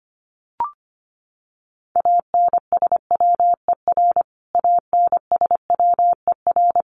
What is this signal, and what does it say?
Morse code, heard as a keyed tone, about 25 wpm, 700 Hz. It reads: ANSWER ANSWER
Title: Top 300 Words Repeated 25wpm